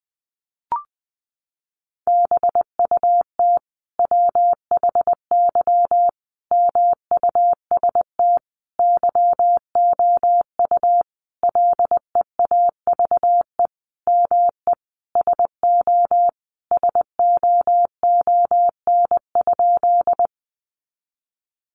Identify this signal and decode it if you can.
Morse code, heard as a keyed tone, about 20 wpm, 700 Hz. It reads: BUT WHY MUST YOU LEAVE ME SO SOON?
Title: Sentences from Top 400 Words 20wpm